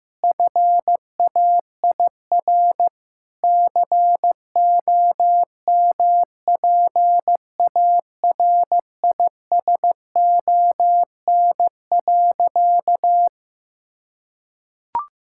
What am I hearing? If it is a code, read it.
Morse code, heard as a keyed tone, about 16 wpm, 700 Hz. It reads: FAIR COMPARISON.
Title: Sentences from Top 1600 Words 15wpm